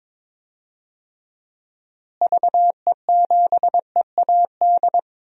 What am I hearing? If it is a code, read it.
Morse code, heard as a keyed tone, about 22 wpm, 700 Hz. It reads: VE7EAD
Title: DXCC-100 Call Signs - 6 Characters - Repeated 3x - 22wpm